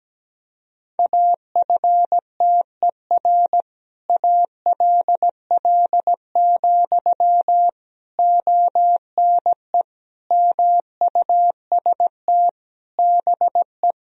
Morse code, heard as a keyed tone, about 17 wpm, 700 Hz. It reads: AFTER ALL, ONE MUST BE
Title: Sentences from Top 1000 Words - Review Part 3 - 17wpm